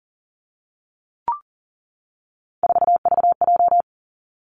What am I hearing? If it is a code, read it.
Morse code, heard as a keyed tone, about 38 wpm, 700 Hz. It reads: 4VJ
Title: Three Challenging Letter-Number Combinations 40wpm